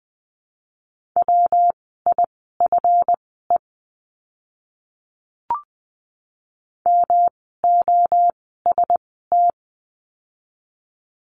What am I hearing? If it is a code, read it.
Morse code, heard as a keyed tone, about 20 wpm, spaced as for 14 wpm, 700 Hz. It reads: WIFE MOST
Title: 139 Lesson 23f - Letter f - Words 20wpm